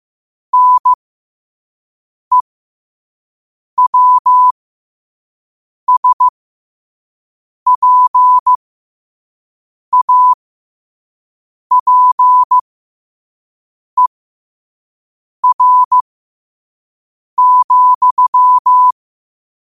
Morse code, heard as a keyed tone, about 15 wpm, spaced as for 5 wpm, 1000 Hz. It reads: NEWSPAPER,
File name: qotdfiles / 5wpm